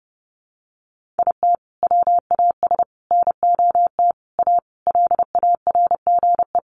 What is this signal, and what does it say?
Morse code, heard as a keyed tone, about 30 wpm, 700 Hz. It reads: IT WAS NOT A LARGE
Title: Sentences from Top 1400 Words 30wpm